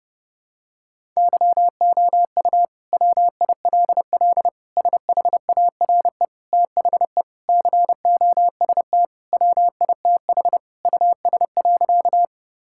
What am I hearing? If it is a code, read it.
Morse code, heard as a keyed tone, about 30 wpm, 700 Hz. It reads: YOU WILL SHARE THE COST WITH US.
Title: Sentences from Top 1000 Words - Review Part 6 - 30wpm